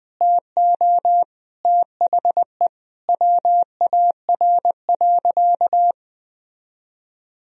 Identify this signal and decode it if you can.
Morse code, heard as a keyed tone, about 20 wpm, 700 Hz. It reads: TO THE WAR.
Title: Sentences from Top 400 Words and Abbreviations 20wpm